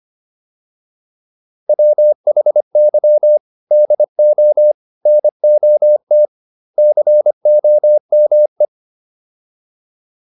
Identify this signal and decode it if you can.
Morse code, heard as a keyed tone, about 25 wpm, 600 Hz. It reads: WHY DO NOT COME